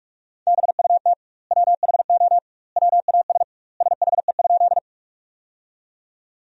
Morse code, heard as a keyed tone, about 46 wpm, 700 Hz. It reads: BUT WHO WAS SHE?
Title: 2x Word-Spacing - Sentences from Top 100 Words - 45wpm